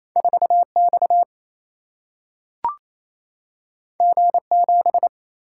Morse code, heard as a keyed tone, about 28 wpm, 700 Hz. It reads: 4 X G 7
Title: Single-Letter Single-Number 28wpm